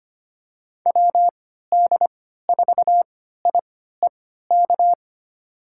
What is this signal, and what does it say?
Morse code, heard as a keyed tone, about 25 wpm, spaced as for 14 wpm, 700 Hz. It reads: WD4IEK
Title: USA Call Signs - 6 Characters - 3x Character Spacing - 25wpm